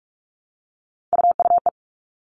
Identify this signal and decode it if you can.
Morse code, heard as a keyed tone, about 44 wpm, 700 Hz. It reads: UUE
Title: All Cut Numbers - 3-Digit 45wpm